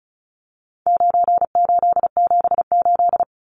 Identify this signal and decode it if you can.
Morse code, heard as a keyed tone, about 34 wpm, 700 Hz. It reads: 9878
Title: Numbers - 4-Digit 35wpm